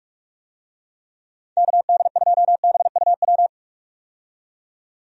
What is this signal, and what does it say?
Morse code, heard as a keyed tone, about 43 wpm, 700 Hz. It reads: KD2BUW
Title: USA Call Signs - 6 Characters - 45wpm